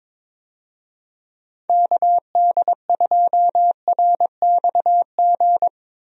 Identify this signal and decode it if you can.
Morse code, heard as a keyed tone, about 22 wpm, 700 Hz. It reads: KD2RXG
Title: USA Call Signs - 6 Characters - Repeated 2x - 22wpm